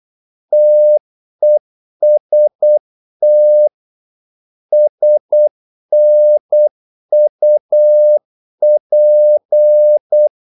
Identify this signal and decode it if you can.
Morse code, heard as a keyed tone, about 8 wpm, 600 Hz. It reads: TEST SNUP
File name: Korte ord DK 8wpm